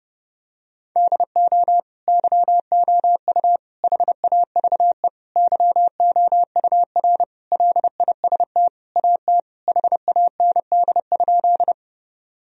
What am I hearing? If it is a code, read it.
Morse code, heard as a keyed tone, about 32 wpm, 700 Hz. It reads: DO YOU HAVE YOUR LIST AT HAND?
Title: Sentences from Top 1000 Words - Review Part 2 - 30wpm